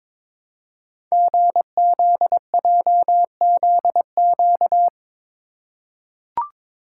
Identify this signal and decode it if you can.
Morse code, heard as a keyed tone, about 22 wpm, 700 Hz. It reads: GZJZQ
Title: Code Groups - 5 Letter:Numbers - 22wpm